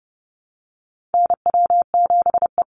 Morse code, heard as a keyed tone, about 30 wpm, 700 Hz. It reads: NW7E
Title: NA SOTA Chaser Callsigns 30wpm